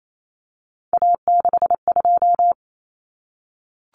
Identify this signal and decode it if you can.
Morse code, heard as a keyed tone, about 28 wpm, 700 Hz. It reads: A62
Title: Common Cut Numbers - 3-Digit 28wpm